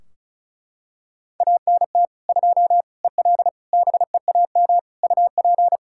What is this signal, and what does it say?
Morse code, heard as a keyed tone, about 34 wpm, 700 Hz. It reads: ANT 2 EL BEAM UP
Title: K1USN SST QSO Practice 35wpm